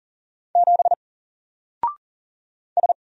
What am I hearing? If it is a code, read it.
Morse code, heard as a keyed tone, about 40 wpm, 700 Hz. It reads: 7 S
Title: Single Letter-Number Alt 40wpm